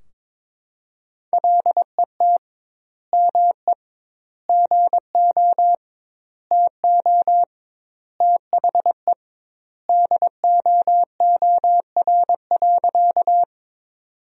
Morse code, heard as a keyed tone, about 22 wpm, 700 Hz. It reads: LET ME GO TO THE DOOR.
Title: 2x Word-Spacing - Sentences from Top 300 Words - 22wpm